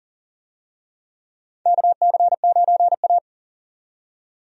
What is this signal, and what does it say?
Morse code, heard as a keyed tone, about 40 wpm, 700 Hz. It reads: KC9A
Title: USA Call Signs - 4 Characters - 40wpm